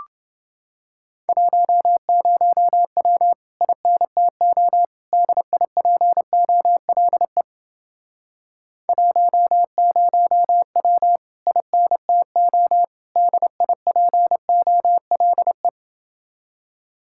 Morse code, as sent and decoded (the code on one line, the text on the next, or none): .---- ----- .-- / .. -. - --- / -.. .. .--. --- .-.. / .---- ----- .-- / .. -. - --- / -.. .. .--. --- .-.. .
10W INTO DIPOLE 10W INTO DIPOLE